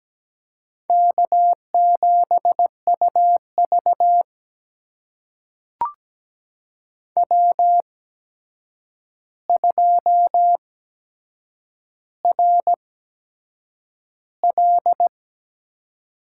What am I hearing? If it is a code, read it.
Morse code, heard as a keyed tone, about 17 wpm, spaced as for 7 wpm, 700 Hz. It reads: K7UV W2RL
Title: USA Call Signs - 4 Characters - 8x Character Spacing - 17wpm